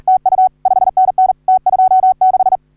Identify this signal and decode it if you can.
Morse code, heard as a keyed tone, about 40 wpm, 750 Hz. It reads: TU 5NN T26
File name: CW / HA3OU RHOBCW REC